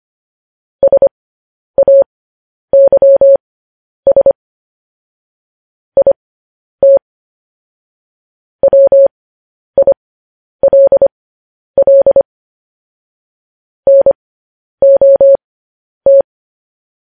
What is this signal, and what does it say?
Morse code, heard as a keyed tone, about 25 wpm, spaced as for 10 wpm, 550 Hz. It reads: SAYS IT WILL NOT
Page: news-headlines-in-morse-code-at-10-wpm